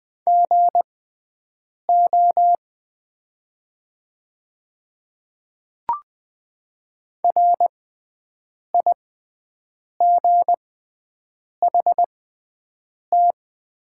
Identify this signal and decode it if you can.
Morse code, heard as a keyed tone, about 20 wpm, spaced as for 7 wpm, 700 Hz. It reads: GO RIGHT
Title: 6x Character Spacing - Top 200 Words - 20wpm